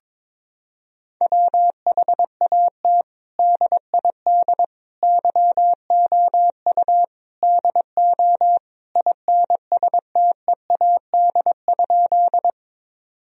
Morse code, heard as a keyed tone, about 21 wpm, 700 Hz. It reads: WHAT DID YOU DO INSTEAD?
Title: Sentences from Top 1000 Words - Review Part 4 - 22wpm